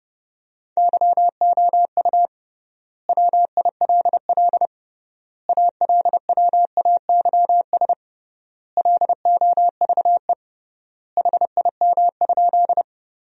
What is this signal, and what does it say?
Morse code, heard as a keyed tone, about 30 wpm, 700 Hz. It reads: YOU WILL ALWAYS LOVE HIM?
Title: 3x Word-Spacing - Sentences from Top 400 Words - 30wpm